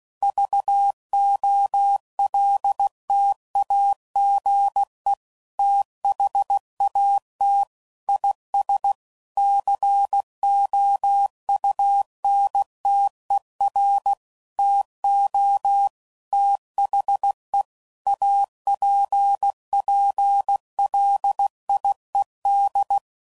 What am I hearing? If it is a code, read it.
Morse code, heard as a keyed tone, about 16 wpm, 800 Hz. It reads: VOLTAGE THAT IS COUNTER TO THE APPLIED
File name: CW Code Groups @15WPM / Technical Subjects / XL